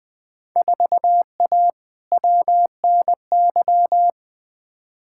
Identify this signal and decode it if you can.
Morse code, heard as a keyed tone, about 20 wpm, 700 Hz. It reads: 4A WNY
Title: Field Day Exchanges Encore - 20wpm